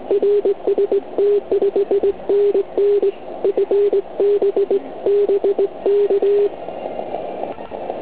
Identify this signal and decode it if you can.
Morse code, heard as a keyed tone, about 19 wpm, 400 Hz. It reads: RST 5NN FB BK
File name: Archiv / CUC129 / CUC129 pileup